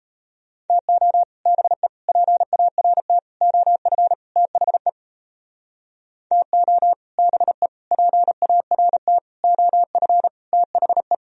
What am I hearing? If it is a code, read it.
Morse code, heard as a keyed tone, about 35 wpm, 700 Hz. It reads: TO BE PART OF THE TO BE PART OF THE
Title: Speed-Racing - Sets of 5 Words 28wpm